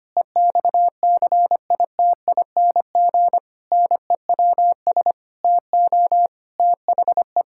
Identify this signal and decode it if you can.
Morse code, heard as a keyed tone, about 25 wpm, 700 Hz. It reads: EXCITING NEWS TO THE
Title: Pangram Challenge v2 - 25wpm